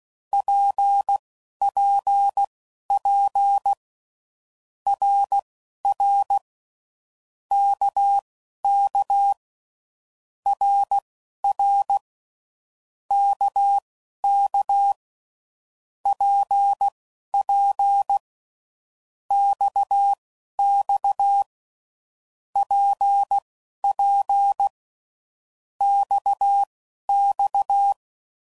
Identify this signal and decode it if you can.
Morse code, heard as a keyed tone, about 16 wpm, spaced as for 11 wpm, 800 Hz. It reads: PPP RR KK RR KK PP XX PP XX
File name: Groups / Basic Letter Groups / RKPX